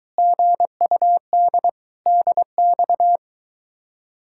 Morse code, heard as a keyed tone, about 23 wpm, 700 Hz. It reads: GUD DX